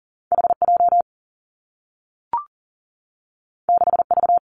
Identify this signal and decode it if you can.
Morse code, heard as a keyed tone, about 41 wpm, 700 Hz. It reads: HJ 6V
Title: Two Challenging Letter-Number Combinations 40wpm